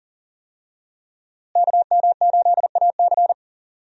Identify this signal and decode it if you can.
Morse code, heard as a keyed tone, about 40 wpm, 700 Hz. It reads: KM8AC
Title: USA Call Signs - 5 Characters - Repeated 3x - 40wpm